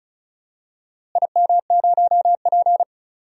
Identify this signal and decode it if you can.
Morse code, heard as a keyed tone, about 36 wpm, 700 Hz. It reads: IM0P
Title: DXCC-100 Call Signs - 4 Characters - Repeated 3x - 35wpm